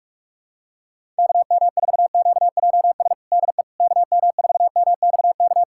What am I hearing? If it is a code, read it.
Morse code, heard as a keyed tone, about 47 wpm, 700 Hz. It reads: KM4QJS DE KM4MXK
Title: USA Callsign DE Callsign - 6 Characters - 45wpm